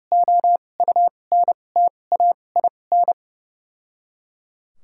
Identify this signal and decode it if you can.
Morse code, heard as a keyed tone, about 30 wpm, spaced as for 22 wpm, 700 Hz. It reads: OUNTAIN
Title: 2x Character Spacing - Top 300 Words - 30wpm